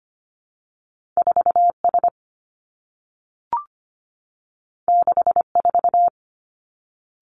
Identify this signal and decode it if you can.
Morse code, heard as a keyed tone, about 25 wpm, 700 Hz. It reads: 4S 64
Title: Two Challenging Letter-Number Combinations 25wpm